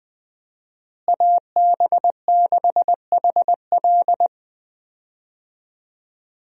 Morse code, heard as a keyed tone, about 20 wpm, 700 Hz. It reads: AB6HL
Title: USA Call Signs - 5 Characters - Repeated 3x - 20wpm